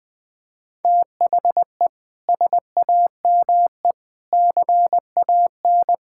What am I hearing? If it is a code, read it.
Morse code, heard as a keyed tone, about 20 wpm, 700 Hz. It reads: THE SAME CAN
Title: Sets of 6 Words - 20wpm